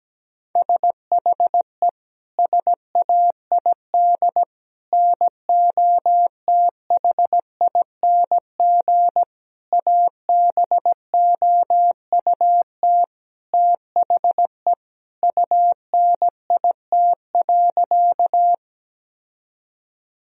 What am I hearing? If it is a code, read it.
Morse code, heard as a keyed tone, about 17 wpm, 700 Hz. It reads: SHE SAID NOTHING ABOUT THE UNIT.
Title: Sentences from Top 1000 Words - Review Part 2 - 17wpm